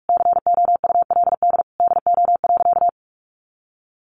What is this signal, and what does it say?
Morse code, heard as a keyed tone, about 44 wpm, 700 Hz. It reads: COULD DO.
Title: Sentences from Top 700 Words 45wpm